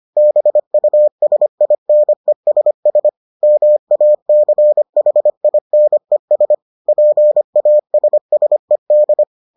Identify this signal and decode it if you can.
Morse code, heard as a keyed tone, about 25 wpm, 600 Hz. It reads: BUSINESS MACHINES PASSED